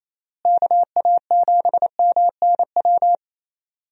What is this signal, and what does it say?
Morse code, heard as a keyed tone, about 28 wpm, 700 Hz. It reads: KA7MNW
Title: USA Call Signs - 6 Characters - Repeated 3x - 28wpm